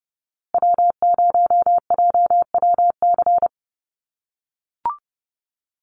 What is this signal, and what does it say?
Morse code, heard as a keyed tone, about 30 wpm, 700 Hz. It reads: W0JWC
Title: USA Call Signs Repeated 2x 30wpm